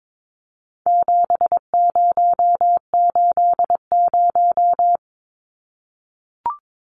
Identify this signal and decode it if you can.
Morse code, heard as a keyed tone, about 22 wpm, 700 Hz. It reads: 7080
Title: Numbers - 4-Digit 22wpm